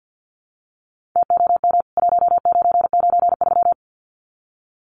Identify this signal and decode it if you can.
Morse code, heard as a keyed tone, about 48 wpm, 700 Hz. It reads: TOM 1993
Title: CWT Exchange 50wpm